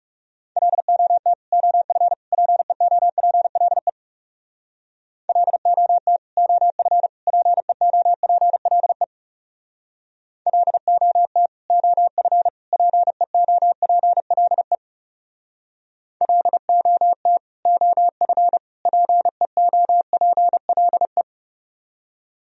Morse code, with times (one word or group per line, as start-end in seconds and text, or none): 0.6-1.3 LOT
1.5-2.1 OF
2.3-3.9 PEOPLE
5.3-6.2 LOT
6.4-7.1 OF
7.3-9.0 PEOPLE
10.5-11.5 LOT
11.7-12.5 OF
12.7-14.8 PEOPLE
16.2-17.4 LOT
17.7-18.6 OF
18.9-21.2 PEOPLE